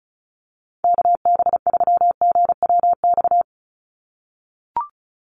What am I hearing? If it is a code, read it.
Morse code, heard as a keyed tone, about 34 wpm, 700 Hz. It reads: KB3GWX
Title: USA Call Signs 35wpm